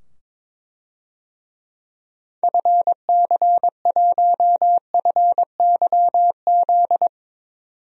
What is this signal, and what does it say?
Morse code, heard as a keyed tone, about 22 wpm, 700 Hz. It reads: FC1FYZ
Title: DXCC-100 Call Signs - 6 Characters - 22wpm